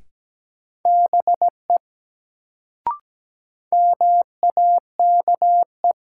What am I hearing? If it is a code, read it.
Morse code, heard as a keyed tone, about 17 wpm, 700 Hz. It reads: BE MAKE
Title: Sets of 2 Words - New Word Drill - 17wpm